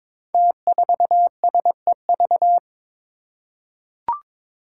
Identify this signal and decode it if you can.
Morse code, heard as a keyed tone, about 22 wpm, 700 Hz. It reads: T4SEV
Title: Five Challenging Letter-Number Combinations 22wpm